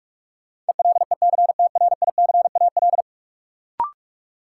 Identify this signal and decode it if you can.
Morse code, heard as a keyed tone, about 45 wpm, 700 Hz. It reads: ELECTRICAL
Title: Sentences from Top 1000 Words - Review Part 6 - New Word Drill - 45wpm